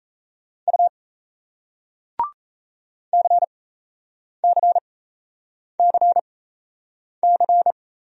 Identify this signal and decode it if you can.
Morse code, heard as a keyed tone, about 40 wpm, spaced as for 8 wpm, 700 Hz. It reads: U CCCC